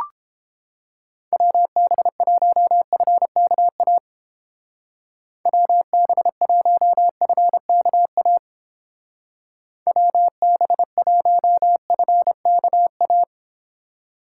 Morse code, heard as a keyed tone, about 29 wpm, 700 Hz. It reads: WB1FKA WB1FKA WB1FKA